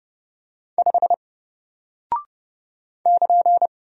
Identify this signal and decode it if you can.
Morse code, heard as a keyed tone, about 30 wpm, 700 Hz. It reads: 5 <KN>
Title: Single Character (Letter:Number:Punctuation:Prosign) 30wpm